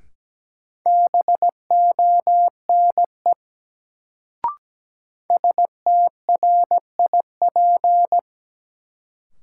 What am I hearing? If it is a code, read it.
Morse code, heard as a keyed tone, about 17 wpm, 700 Hz. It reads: BONE STRIP
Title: Sentences from Top 1600 Words - New Word Drill Alt - 17wpm